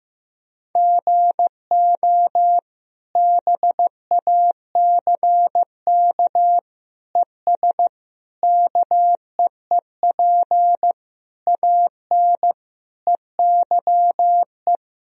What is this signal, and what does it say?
Morse code, heard as a keyed tone, about 15 wpm, 700 Hz. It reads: GO BACK ES KEEP AN EYE
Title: Sentences from Top 300 Words and Abbreviations 15wpm